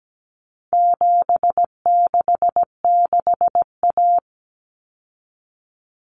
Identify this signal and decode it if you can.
Morse code, heard as a keyed tone, about 17 wpm, 700 Hz. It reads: 766A
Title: Common Cut Numbers - 4-Digit 17wpm